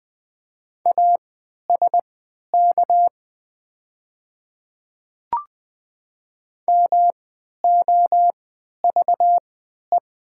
Morse code, heard as a keyed tone, about 20 wpm, spaced as for 11 wpm, 700 Hz. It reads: ASK MOVE
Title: 3x Character Spacing - Top 300 Words - 20wpm